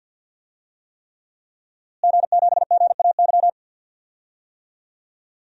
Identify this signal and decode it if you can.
Morse code, heard as a keyed tone, about 47 wpm, 700 Hz. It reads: G7GAY
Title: DXCC-100 Call Signs - 5 Characters - Repeated 3x - 50wpm